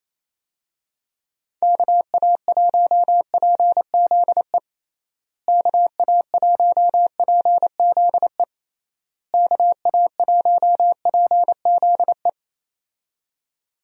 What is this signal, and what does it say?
Morse code, heard as a keyed tone, about 27 wpm, 700 Hz. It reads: KA1PZE KA1PZE KA1PZE
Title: USA Call Signs - 6 Characters - Repeated 3x - 28wpm